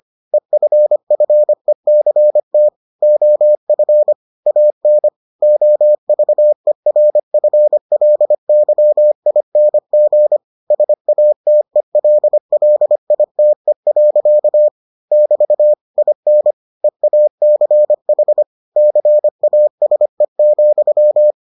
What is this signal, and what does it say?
Morse code, heard as a keyed tone, about 25 wpm, 600 Hz. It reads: EFFECT OF AN OVERFLYING SATELLITE. <BT> IN EACH CASE,